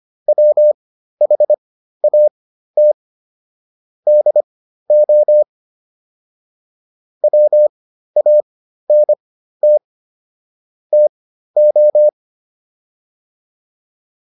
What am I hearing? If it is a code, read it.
Morse code, heard as a keyed tone, about 25 wpm, spaced as for 13 wpm, 600 Hz. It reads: WHAT DO WANT TO